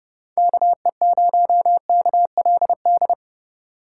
Morse code, heard as a keyed tone, about 30 wpm, 700 Hz. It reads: KE0KLD
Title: USA Call Signs - 6 Characters - 4x Character Spacing - 30wpm